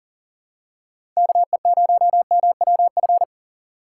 Morse code, heard as a keyed tone, about 37 wpm, 700 Hz. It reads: KE0MWF
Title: USA Call Signs - 3x Character Spacing - 40wpm